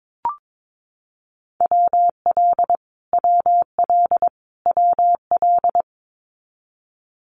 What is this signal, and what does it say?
Morse code, heard as a keyed tone, about 22 wpm, 700 Hz. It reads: WL WL WL
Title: Abbreviations Repeated 3x - 22wpm